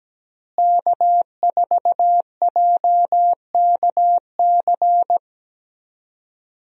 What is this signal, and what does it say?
Morse code, heard as a keyed tone, about 17 wpm, 700 Hz. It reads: K4JKC
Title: USA Call Signs - 5 Characters - Repeated 3x - 17wpm